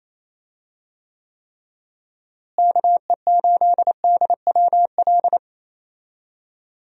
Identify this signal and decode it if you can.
Morse code, heard as a keyed tone, about 28 wpm, 700 Hz. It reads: KE8DWL
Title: USA Call Signs - 6 Characters - Repeated 3x - 28wpm